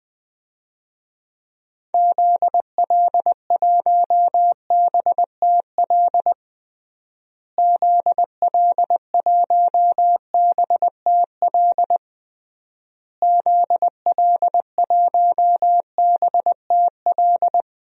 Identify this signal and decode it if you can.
Morse code, heard as a keyed tone, about 21 wpm, 700 Hz. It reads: ZL1BTL ZL1BTL ZL1BTL
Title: DXCC-100 Call Signs - 6 Characters - Repeated 3x - 20wpm